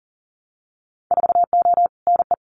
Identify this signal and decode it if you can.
Morse code, heard as a keyed tone, about 41 wpm, 700 Hz. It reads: 4O NE
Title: Winter Field Day Exchanges Encore 40wpm